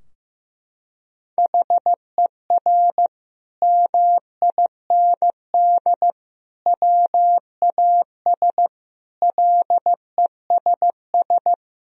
Morse code, heard as a keyed tone, about 15 wpm, 700 Hz. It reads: HER MIND WAS LESS
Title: Sentences from Top 1000 Words - Review Part 2 - 15wpm